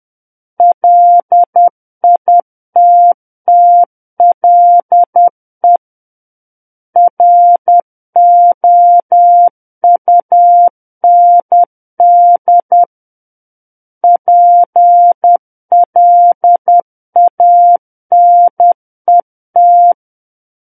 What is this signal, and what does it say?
Morse code, heard as a keyed tone, about 10 wpm, 700 Hz. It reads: LITTLE ROUND PLANET
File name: Book One / 01-Chapter 1 - 10 WPM